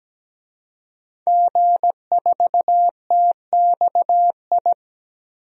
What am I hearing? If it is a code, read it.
Morse code, heard as a keyed tone, about 17 wpm, 700 Hz. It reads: G4TXI
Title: DXCC-100 Call Signs - 5 Characters - 17wpm